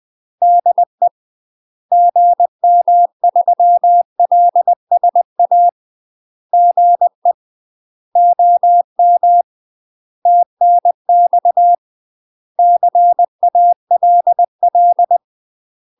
Morse code, heard as a keyed tone, about 20 wpm, 700 Hz. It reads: DE GM3LSA GE OM TNX CALL
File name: UK MASTER QSOs TRAINING / 20WPM / UK MASTER QSOs 0026